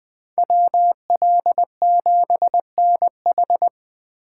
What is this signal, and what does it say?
Morse code, heard as a keyed tone, about 20 wpm, 700 Hz. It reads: WL7NH
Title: Speed Racing - USA Call Signs 17wpm